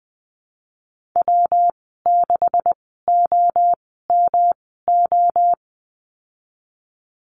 Morse code, heard as a keyed tone, about 20 wpm, spaced as for 15 wpm, 700 Hz. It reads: W6OMO
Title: 161 Lesson 26g - Letter g - Call Signs 20wpm